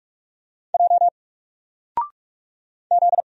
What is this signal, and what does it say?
Morse code, heard as a keyed tone, about 44 wpm, 700 Hz. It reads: J Z
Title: Single Letter-Number Alt 45wpm